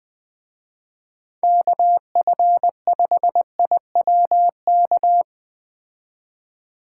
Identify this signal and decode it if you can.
Morse code, heard as a keyed tone, about 20 wpm, 700 Hz. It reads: KF5IWK
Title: Speed Racing - USA 2x3 Call Signs 17wpm